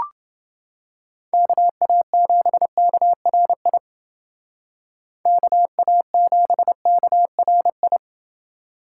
Morse code, heard as a keyed tone, about 29 wpm, 700 Hz. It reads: KA7KRI KA7KRI